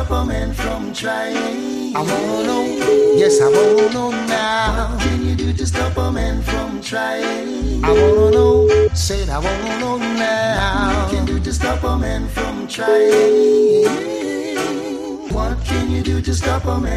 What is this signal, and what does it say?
Ringback tone (Europe, 425 Hz), with noise.